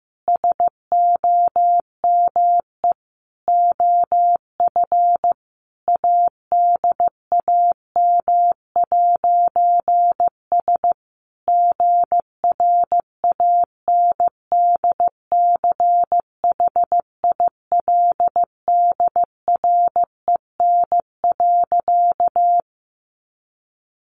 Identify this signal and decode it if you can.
Morse code, heard as a keyed tone, about 15 wpm, 700 Hz. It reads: SOME OF ADAM'S GRANDCHILDREN.
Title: A Sound Of Thunder 15wpm